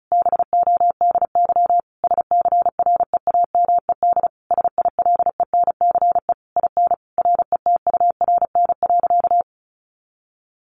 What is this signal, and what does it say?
Morse code, heard as a keyed tone, about 34 wpm, 700 Hz. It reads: BODY SCREAMED SILENCE IN RETURN.
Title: A Sound Of Thunder 35wpm